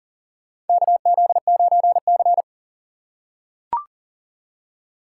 Morse code, heard as a keyed tone, about 40 wpm, 700 Hz. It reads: KZ9C
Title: USA Call Signs - 4 Characters - 2x Character Spacing - 40wpm